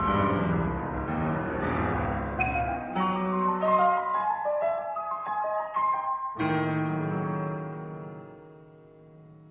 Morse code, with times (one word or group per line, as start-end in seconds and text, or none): none